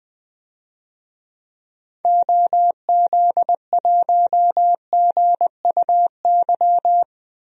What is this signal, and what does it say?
Morse code, heard as a keyed tone, about 20 wpm, 700 Hz. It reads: OZ1GUY
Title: DXCC-100 Call Signs - 6 Characters - Repeated 3x - 20wpm